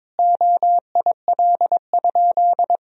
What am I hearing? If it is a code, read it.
Morse code, heard as a keyed tone, about 22 wpm, 700 Hz. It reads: OIL?
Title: Sentences from Top 1000 Words - Review Part 4 - 22wpm